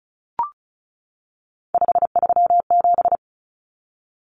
Morse code, heard as a keyed tone, about 34 wpm, 700 Hz. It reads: 537
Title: Common Cut Numbers - 3-Digit 35wpm